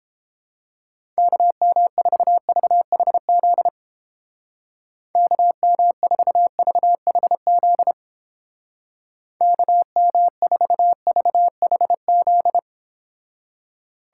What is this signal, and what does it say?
Morse code, heard as a keyed tone, about 31 wpm, 700 Hz. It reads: KM4VHZ KM4VHZ KM4VHZ